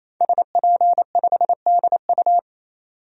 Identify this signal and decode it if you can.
Morse code, heard as a keyed tone, about 28 wpm, 700 Hz. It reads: SP5DU
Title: DXCC-100 Call Signs - 5 Characters - 28wpm